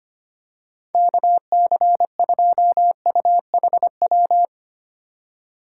Morse code, heard as a keyed tone, about 25 wpm, 700 Hz. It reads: KC2UHW
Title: USA Call Signs - 4x Character Spacing - 25wpm